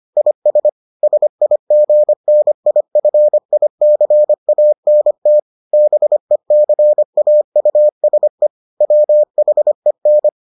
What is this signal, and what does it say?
Morse code, heard as a keyed tone, about 25 wpm, 600 Hz. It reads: IS SIGNIFICANT BECAUSE WHEN